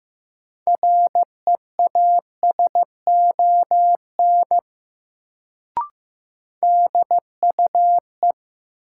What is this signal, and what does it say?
Morse code, heard as a keyed tone, about 15 wpm, 700 Hz. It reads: REASON DUE
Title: Sets of 5 Words Encore - New Word Drill - 15wpm